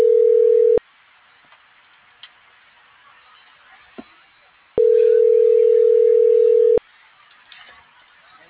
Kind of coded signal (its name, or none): telephone call progress tone